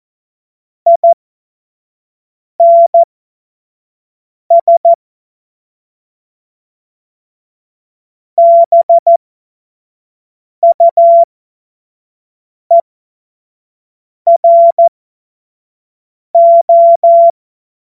Morse code, heard as a keyed tone, about 14 wpm, spaced as for 5 wpm, 680 Hz. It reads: INS BUERO
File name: Morsekurs / Witze / Witz 13. Juni 2023 25 F70